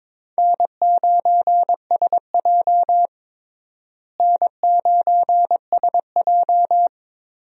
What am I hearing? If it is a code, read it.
Morse code, heard as a keyed tone, about 22 wpm, 700 Hz. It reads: N9SJ N9SJ
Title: USA Call Signs - 4 Characters - Repeated 3x - 22wpm